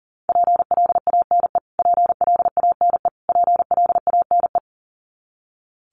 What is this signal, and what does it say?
Morse code, heard as a keyed tone, about 40 wpm, 700 Hz. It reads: PLANE PLANE PLANE
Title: Top 500 Words Repeated 40wpm